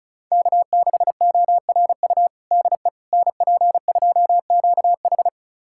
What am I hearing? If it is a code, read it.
Morse code, heard as a keyed tone, about 36 wpm, 700 Hz. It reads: K6ORU DE NP2QH
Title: USA Callsign DE Callsign - 5 Characters - 35wpm